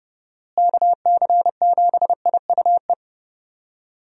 Morse code, heard as a keyed tone, about 30 wpm, 700 Hz. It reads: KC7IUE
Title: USA Call Signs - 6 Characters - Repeated 3x - 30wpm